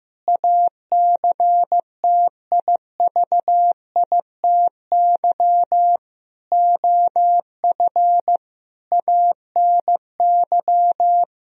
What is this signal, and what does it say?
Morse code, heard as a keyed tone, about 15 wpm, 700 Hz. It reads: ACTIVITY OF ANY
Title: Sentences from Top 1000 Words - Review Part 4 - 15wpm